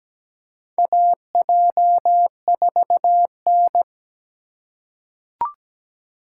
Morse code, heard as a keyed tone, about 17 wpm, 700 Hz. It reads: AJ4N
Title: USA Call Signs - 4 Characters - 3x Character Spacing - 17wpm